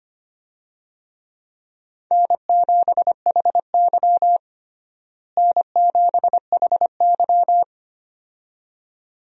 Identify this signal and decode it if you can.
Morse code, heard as a keyed tone, about 25 wpm, 700 Hz. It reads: N7HY N7HY
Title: USA Call Signs - 4 Characters - Repeated 2x - 25wpm